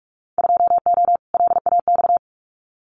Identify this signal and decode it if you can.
Morse code, heard as a keyed tone, about 44 wpm, 700 Hz. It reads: 2O LAX
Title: Winter Field Day Exchanges 45wpm